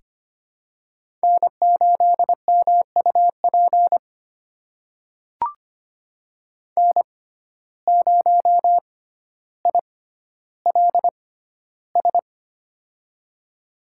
Morse code, heard as a keyed tone, about 25 wpm, spaced as for 13 wpm, 700 Hz. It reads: N8MUP N0ILS
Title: USA Call Signs - 5 Characters - 6x Character Spacing - 25wpm